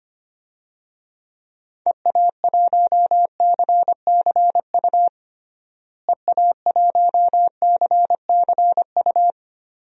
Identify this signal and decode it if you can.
Morse code, heard as a keyed tone, about 25 wpm, 700 Hz. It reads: EA1CCU EA1CCU
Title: DXCC-100 Call Signs - 6 Characters - Repeated 3x - 25wpm